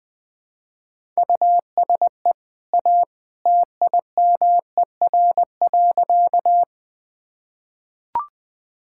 Morse code, heard as a keyed tone, about 21 wpm, 700 Hz. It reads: USE A TIMER.
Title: Easy Sentences with Hints - 20wpm